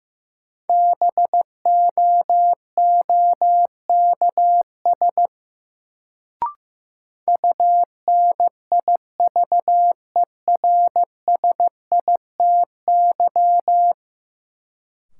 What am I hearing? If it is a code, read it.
Morse code, heard as a keyed tone, about 15 wpm, 700 Hz. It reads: BOOKS UNIVERSITY
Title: Sentences from Top 1000 Words - Review Part 4 - New Word Drill - 15wpm